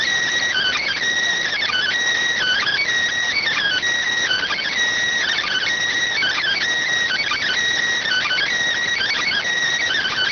Emergency Alert System (SAME) tones (header bursts), heard as harsh buzzing bursts. It is heard from 0.6 to 10.1 s.